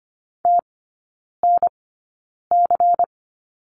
Morse code, heard as a keyed tone, about 25 wpm, spaced as for 9 wpm, 700 Hz. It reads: TNC
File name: course / 25:10wpm / 093 Lesson 16e - Letter c - Three Characters 25wpm